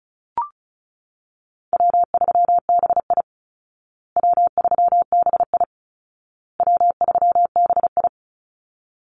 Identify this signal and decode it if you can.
Morse code, heard as a keyed tone, about 34 wpm, 700 Hz. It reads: W3BI W3BI W3BI
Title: Call Signs Repeated 2x - 35wpm